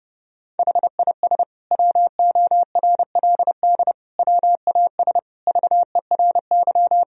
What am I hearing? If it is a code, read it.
Morse code, heard as a keyed tone, about 30 wpm, 700 Hz. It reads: HIS WORLD WAS VERY
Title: Sentences from Top 1000 Words - Review Part 1 - 30wpm